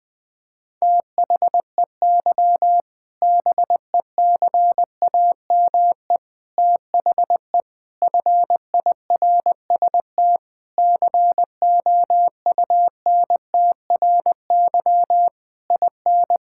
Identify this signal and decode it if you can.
Morse code, heard as a keyed tone, about 20 wpm, 700 Hz. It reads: THEY BECAME THE FIRST COUNTRY IN
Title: Sentences with 8-Word Phrases - 20wpm